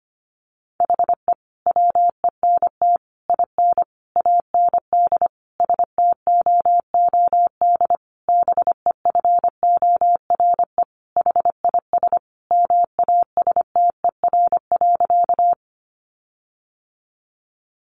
Morse code, heard as a keyed tone, about 25 wpm, 700 Hz. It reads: HE WENT IN AND STOOD BEFORE HIS MASTER.
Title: Sentences from Top 1300 Words 25wpm